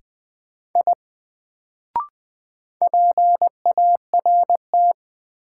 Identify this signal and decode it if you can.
Morse code, heard as a keyed tone, about 20 wpm, 700 Hz. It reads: I PART
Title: Sentences from Top 200 Words - New Word Drill - 20wpm